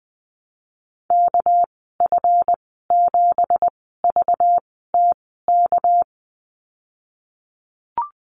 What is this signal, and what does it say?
Morse code, heard as a keyed tone, about 20 wpm, spaced as for 15 wpm, 700 Hz. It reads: KF7VTK
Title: 231 Lesson 37g - Letter x - Call Signs 20wpm